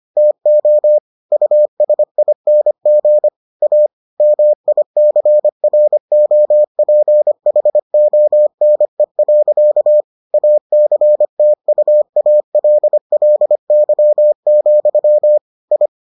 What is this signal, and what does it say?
Morse code, heard as a keyed tone, about 25 wpm, 600 Hz. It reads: TO USING A MICROPHONE. ACTUALLY, I